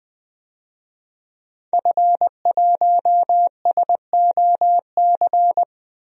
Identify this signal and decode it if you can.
Morse code, heard as a keyed tone, about 20 wpm, 700 Hz. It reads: F1SOC
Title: DXCC-100 Call Signs - 5 Characters - Repeated 2x - 20wpm